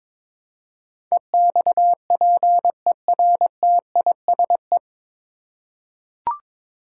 Morse code, heard as a keyed tone, about 22 wpm, 700 Hz. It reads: EXPERTISE
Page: Morse Code Ninja practice 22wpm